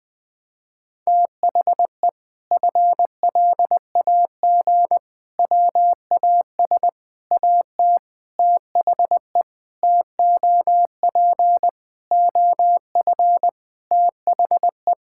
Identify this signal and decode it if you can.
Morse code, heard as a keyed tone, about 20 wpm, 700 Hz. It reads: THE FLAG WAS AT THE TOP OF THE
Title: Sentences with 5-Word Phrases - 20wpm